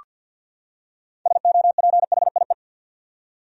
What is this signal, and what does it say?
Morse code, heard as a keyed tone, about 47 wpm, 700 Hz. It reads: SOPHIE
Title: USA Common Names - 50wpm